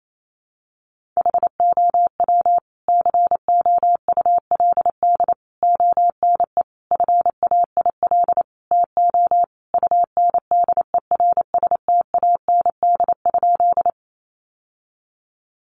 Morse code, heard as a keyed tone, about 28 wpm, 700 Hz. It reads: HOW COULD ONE FAIL TO UNDERSTAND?
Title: Sentences from Top 1100 Words 28wpm